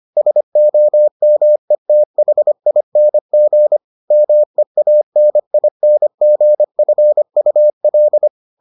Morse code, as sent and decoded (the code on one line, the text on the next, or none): ... --- -- . - .... .. -. --. / -- . .- -. .. -. --. ..-. ..- .-..
SOMETHING MEANINGFUL